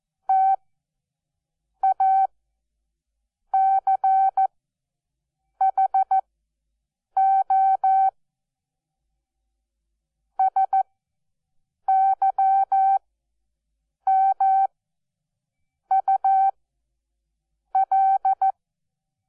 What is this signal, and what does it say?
Morse code, heard as a keyed tone, about 14 wpm, spaced as for 6 wpm, 770 Hz. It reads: TACHO SYMUL
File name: Onlinelehrgang / morsen / DJ4UF Morsen12